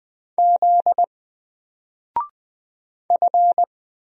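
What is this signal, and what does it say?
Morse code, heard as a keyed tone, about 20 wpm, 700 Hz. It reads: Z F
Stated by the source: Single letter and number Morse drill at 20 wpm, from N4LCD.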